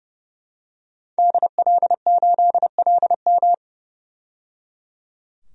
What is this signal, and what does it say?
Morse code, heard as a keyed tone, about 30 wpm, 700 Hz. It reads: DL8LM
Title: DXCC-100 Call Signs - 5 Characters - 30wpm